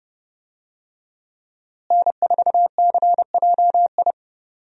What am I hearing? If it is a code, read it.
Morse code, heard as a keyed tone, about 30 wpm, 700 Hz. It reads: N4CJI
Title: USA Call Signs - 5 Characters - Repeated 2x - 30wpm